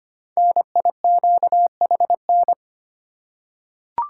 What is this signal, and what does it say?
Morse code, heard as a keyed tone, about 25 wpm, 700 Hz. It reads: NIQHN
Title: Code Groups - 5 Letter:Numbers - 25wpm